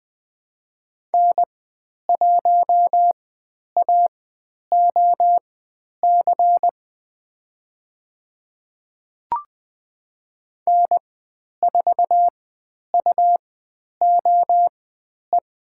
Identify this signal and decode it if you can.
Morse code, heard as a keyed tone, about 20 wpm, spaced as for 10 wpm, 700 Hz. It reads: N1AOC N4UOE
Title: 109 Lesson 18g - Letter m - Call Signs 20wpm